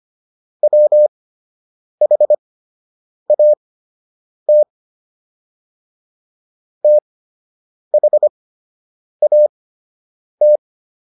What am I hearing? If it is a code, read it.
Morse code, heard as a keyed tone, about 25 wpm, spaced as for 8 wpm, 600 Hz. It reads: WHAT THAT